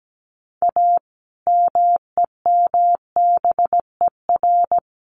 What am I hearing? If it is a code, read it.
Morse code, heard as a keyed tone, about 17 wpm, 700 Hz. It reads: A MEMBER
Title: Sets of 3 Words 17wpm